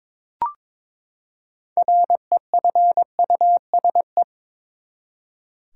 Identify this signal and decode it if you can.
Morse code, heard as a keyed tone, about 21 wpm, 700 Hz. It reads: REFUSE
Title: Six Letter Words - 22wpm